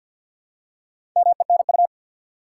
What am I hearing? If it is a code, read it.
Morse code, heard as a keyed tone, about 52 wpm, 700 Hz. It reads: MENU
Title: Four Letter Words - 50wpm